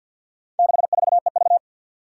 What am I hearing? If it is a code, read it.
Morse code, heard as a keyed tone, about 47 wpm, 700 Hz. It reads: 64EV
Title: Four Challenging Letter-Number Combinations 50wpm